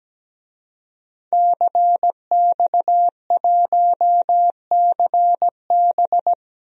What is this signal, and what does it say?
Morse code, heard as a keyed tone, about 17 wpm, 700 Hz. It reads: CX1CB
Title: DXCC-100 Call Signs - 5 Characters - Repeated 3x - 17wpm